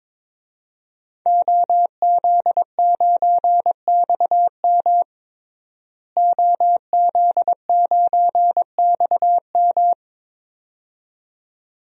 Morse code, heard as a keyed tone, about 22 wpm, 700 Hz. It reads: OZ9XM OZ9XM
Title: DXCC-100 Call Signs - 5 Characters - Repeated 2x - 22wpm